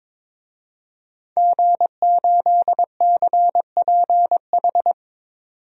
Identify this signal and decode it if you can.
Morse code, heard as a keyed tone, about 21 wpm, 700 Hz. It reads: G8CPH
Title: DXCC-100 Call Signs - 5 Characters - 22wpm